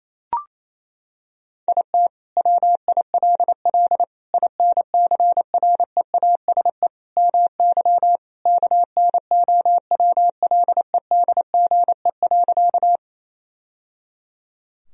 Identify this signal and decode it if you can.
Morse code, heard as a keyed tone, about 28 wpm, 700 Hz. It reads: IT WILL INCREASE MY KNOWLEDGE.
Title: Sentences from Top 1000 Words - Review Part 4 - 28wpm